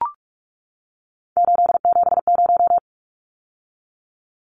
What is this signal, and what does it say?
Morse code, heard as a keyed tone, about 44 wpm, 700 Hz. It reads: 870
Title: Numbers - 3-Digit 45wpm